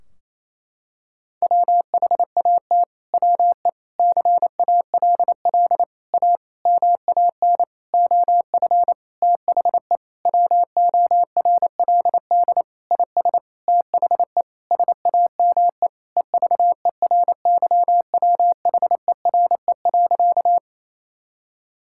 Morse code, heard as a keyed tone, about 28 wpm, 700 Hz. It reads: WHAT WE CALL A MAN OF THE WORLD IS THE SAME EVERYWHERE.
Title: Sentences from Top 1600 Words 28wpm